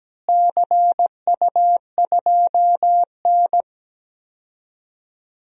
Morse code, heard as a keyed tone, about 17 wpm, 700 Hz. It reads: CU2N
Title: DXCC-100 Call Signs - 4 Characters - Repeated 2x - 17wpm